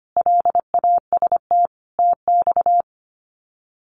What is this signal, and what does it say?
Morse code, heard as a keyed tone, about 25 wpm, 700 Hz. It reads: LAST TX
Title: Common QSO Phrases 25wpm